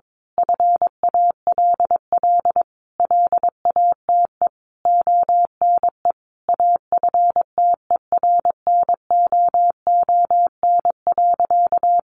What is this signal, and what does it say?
Morse code, heard as a keyed tone, about 22 wpm, 700 Hz. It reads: FALL LATE ONE AFTERNOON.
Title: Sentences from Top 1500 Words 22wpm